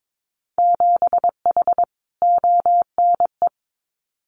Morse code, heard as a keyed tone, about 22 wpm, 700 Hz. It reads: 7H ONE
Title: Winter Field Day Exchanges Encore 22wpm